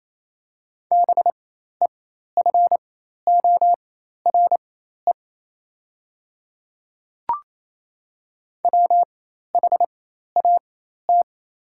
Morse code, heard as a keyed tone, about 28 wpm, spaced as for 13 wpm, 690 Hz. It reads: BEFORE WHAT
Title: 4x Character Spacing - Top 300 Words - 28wpm